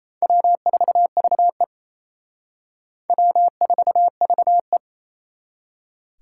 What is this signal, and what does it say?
Morse code, heard as a keyed tone, about 35 wpm, 700 Hz. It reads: W4VE W4VE